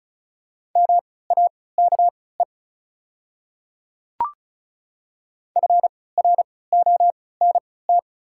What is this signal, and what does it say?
Morse code, heard as a keyed tone, about 34 wpm, spaced as for 20 wpm, 690 Hz. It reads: MAKE FRONT
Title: 3x Character Spacing - Top 500 Words - 35wpm